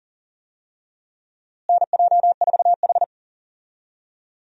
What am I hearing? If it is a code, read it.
Morse code, heard as a keyed tone, about 37 wpm, 700 Hz. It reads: NJ4H
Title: USA Call Signs - 4 Characters - Repeated 2x - 40wpm